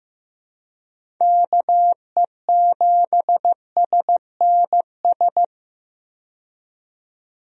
Morse code, heard as a keyed tone, about 15 wpm, 700 Hz. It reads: KE7SNS
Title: USA Call Signs - 6 Characters - 15wpm